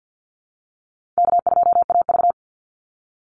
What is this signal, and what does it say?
Morse code, heard as a keyed tone, about 50 wpm, 700 Hz. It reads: K2AV
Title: Call Signs - 50wpm